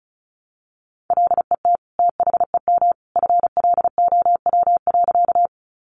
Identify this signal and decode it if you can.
Morse code, heard as a keyed tone, about 34 wpm, 700 Hz. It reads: LET THEM FLOW.
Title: Sentences from Top 1200 Words 35wpm